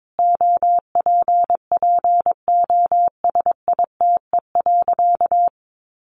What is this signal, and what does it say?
Morse code, heard as a keyed tone, about 22 wpm, 700 Hz. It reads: OPPOSITE.
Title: Sentences from Top 1300 Words 22wpm